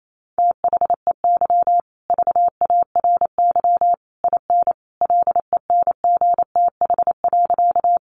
Morse code, heard as a keyed tone, about 28 wpm, 700 Hz. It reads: THEY VARY IN LENGTH.
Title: Sentences from Top 1200 Words 28wpm